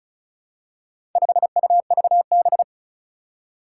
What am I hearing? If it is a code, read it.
Morse code, heard as a keyed tone, about 36 wpm, 700 Hz. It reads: 5UVB
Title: Four Challenging Letter-Number Combinations 35wpm